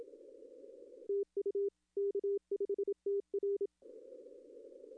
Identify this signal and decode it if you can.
Morse code, heard as a keyed tone, about 27 wpm, 400 Hz. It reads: TU K5TR
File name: contests / arrl-160-k5tr-op-w5tn -0800